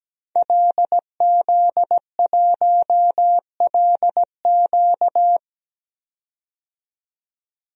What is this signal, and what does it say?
Morse code, heard as a keyed tone, about 17 wpm, 700 Hz. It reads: LZ1LQ
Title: DXCC-100 Call Signs - 5 Characters - 17wpm